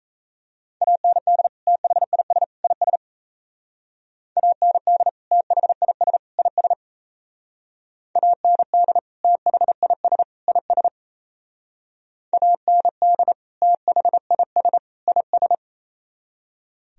Morse code, heard as a keyed tone, about 34 wpm, 700 Hz. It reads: AND THIS IS AND THIS IS AND THIS IS AND THIS IS
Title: Speed-Racing - Sets of 3 Words 28wpm